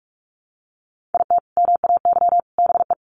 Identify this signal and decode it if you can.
Morse code, heard as a keyed tone, about 44 wpm, 700 Hz. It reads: IT MAY BE
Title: Sets of 3 Words 45wpm